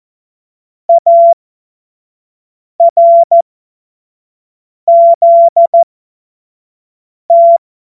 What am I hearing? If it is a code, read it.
Morse code, heard as a keyed tone, about 14 wpm, spaced as for 5 wpm, 680 Hz. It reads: ARZT
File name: Morsekurs / Witze / Witz 12 März 2024 25 F70